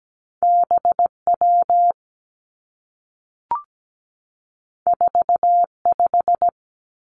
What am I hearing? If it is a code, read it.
Morse code, heard as a keyed tone, about 17 wpm, 700 Hz. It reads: B W 4 5
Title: Two Challenging Letter-Number Combinations 17wpm